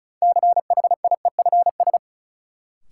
Morse code, heard as a keyed tone, about 36 wpm, 700 Hz. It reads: CHIEFS
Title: Sets of 5 Words Encore - New Word Drill - 35wpm